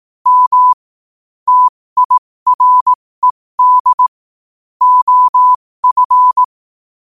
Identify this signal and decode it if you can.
Morse code, heard as a keyed tone, about 18 wpm, spaced as for 15 wpm, 1000 Hz. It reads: M TIRED OF
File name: qotdfiles / 15wpm